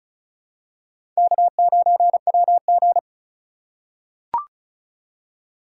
Morse code, heard as a keyed tone, about 34 wpm, 700 Hz. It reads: K9WG
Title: USA Call Signs - 4 Characters - 3x Character Spacing - 35wpm